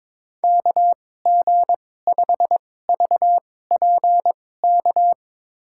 Morse code, heard as a keyed tone, about 22 wpm, spaced as for 16 wpm, 700 Hz. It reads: KG5VPK
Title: USA Call Signs - 2x Character Spacing - 22wpm